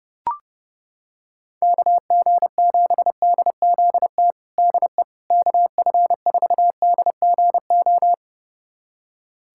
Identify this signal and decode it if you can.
Morse code, heard as a keyed tone, about 32 wpm, 700 Hz. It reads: KG7DZT DE KF4DGO
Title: USA Callsign DE Callsign - 6 Characters - 30wpm